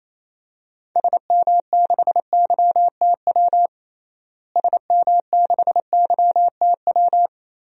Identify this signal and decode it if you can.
Morse code, heard as a keyed tone, about 28 wpm, 700 Hz. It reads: SM6YTW SM6YTW
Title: DXCC-100 Call Signs - 6 Characters - Repeated 2x - 28wpm